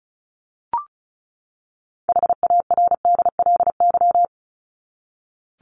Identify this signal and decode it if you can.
Morse code, heard as a keyed tone, about 34 wpm, 700 Hz. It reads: HARDLY
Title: LY Suffix - Top 100 Words with Base Words 35wpm